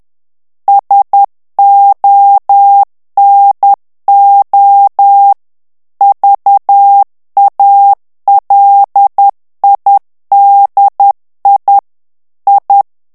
Morse code, heard as a keyed tone, about 10 wpm, 800 Hz. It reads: SONO VALIDI I
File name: ITALIEN texte clair 2 vitesse 10 mots minute